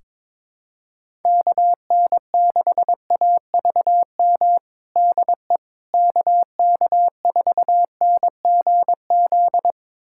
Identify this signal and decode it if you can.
Morse code, heard as a keyed tone, about 22 wpm, 700 Hz. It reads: KN6AVM DE KK4NGZ
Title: USA Callsign DE Callsign 22wpm